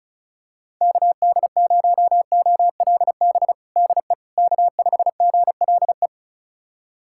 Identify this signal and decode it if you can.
Morse code, heard as a keyed tone, about 34 wpm, 700 Hz. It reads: KD0OLB DE K5GLE
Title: USA Callsign DE Callsign 35wpm